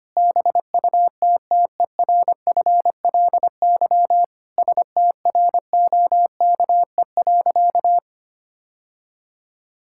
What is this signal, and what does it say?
Morse code, heard as a keyed tone, about 25 wpm, 700 Hz. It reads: BUTTERFLY STROKE.
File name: practice / 25wpm / All-Age Appropriate Jokes - 25wpm